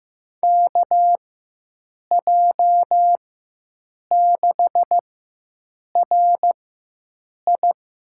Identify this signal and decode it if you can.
Morse code, heard as a keyed tone, about 15 wpm, spaced as for 7 wpm, 700 Hz. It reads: KJ6RI
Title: USA Call Signs - 6 Characters - 4x Character Spacing - 15wpm